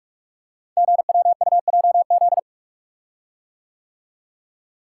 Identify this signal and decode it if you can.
Morse code, heard as a keyed tone, about 44 wpm, 700 Hz. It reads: GWUJZ
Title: Code Groups - 5 Letters - 45wpm